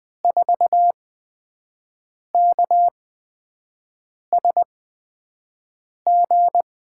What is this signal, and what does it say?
Morse code, heard as a keyed tone, about 20 wpm, spaced as for 5 wpm, 700 Hz. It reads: 4KSG
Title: USA Call Signs - 6 Characters - 8x Character Spacing - 20wpm